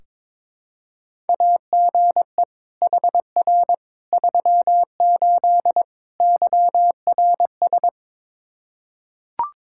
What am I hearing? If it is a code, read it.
Morse code, heard as a keyed tone, about 22 wpm, 700 Hz. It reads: AGE HR 38 YRS
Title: K1USN SST QSO Practice 22wpm